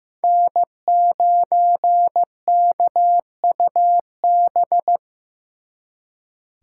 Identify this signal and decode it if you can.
Morse code, heard as a keyed tone, about 18 wpm, 700 Hz. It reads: N9KUB
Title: Speed Racing - USA 1x3 Call Signs 15wpm